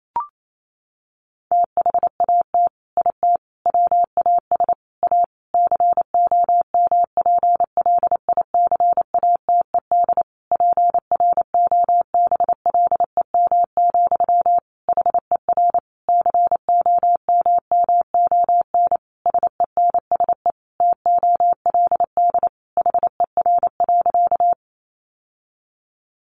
Morse code, heard as a keyed tone, about 28 wpm, 700 Hz. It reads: THAT IT WAS A COMPLICATED PROBLEM, HER COMMON SENSE TOLD HER.
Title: Sentences from Top 1200 Words 28wpm